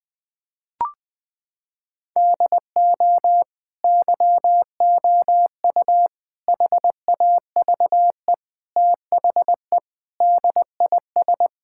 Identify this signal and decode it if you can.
Morse code, heard as a keyed tone, about 20 wpm, 700 Hz. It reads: DO YOU HAVE THE DIS
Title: Sentences from Top 1000 Words - Review Part 6 - 20wpm